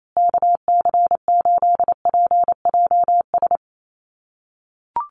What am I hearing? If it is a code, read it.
Morse code, heard as a keyed tone, about 28 wpm, 700 Hz. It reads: KC8PJS
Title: USA Call Signs Repeated 3x 28wpm